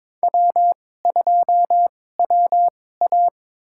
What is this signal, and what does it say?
Morse code, heard as a keyed tone, about 22 wpm, spaced as for 16 wpm, 700 Hz. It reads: W2WA
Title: USA Call Signs - 4 Characters - 2x Character Spacing - 22wpm